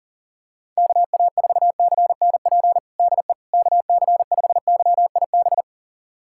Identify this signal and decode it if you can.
Morse code, heard as a keyed tone, about 40 wpm, 700 Hz. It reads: KA4CNP DE KC5YIB
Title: USA Callsign DE Callsign - 6 Characters - 40wpm